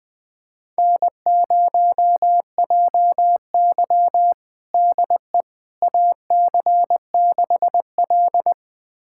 Morse code, heard as a keyed tone, about 21 wpm, 700 Hz. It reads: N0JY DE AC6L
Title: USA Callsign DE Callsign - 4 Characters - 20wpm